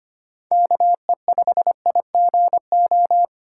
Morse code, heard as a keyed tone, about 25 wpm, 700 Hz. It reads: KE5IGO
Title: USA Call Signs - 6 Characters - 4x Character Spacing - 25wpm